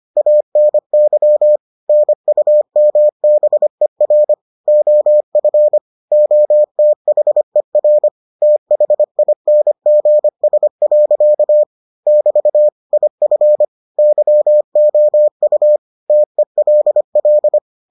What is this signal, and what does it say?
Morse code, heard as a keyed tone, about 25 wpm, 600 Hz. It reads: ANY NUMBER OF OTHER THINGS. <BT> IF YOU TELL